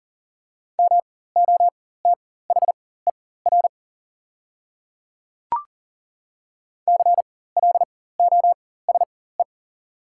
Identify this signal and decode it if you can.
Morse code, heard as a keyed tone, about 40 wpm, spaced as for 19 wpm, 690 Hz. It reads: MOTHER CLOSE
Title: 4x Character Spacing - Top 500 Words - 40wpm